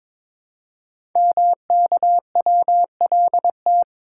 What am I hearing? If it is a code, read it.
Morse code, heard as a keyed tone, about 22 wpm, 700 Hz. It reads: MKWLT
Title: Code Groups - 5 Letter:Numbers - 22wpm